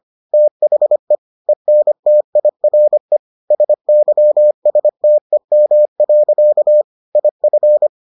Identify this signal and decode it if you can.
Morse code, heard as a keyed tone, about 25 wpm, 600 Hz. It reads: THE ENTIRE SYSTEM. IF